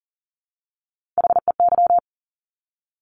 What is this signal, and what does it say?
Morse code, heard as a keyed tone, about 39 wpm, 700 Hz. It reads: HEY
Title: Top Three Letter Combinations in Words 40wpm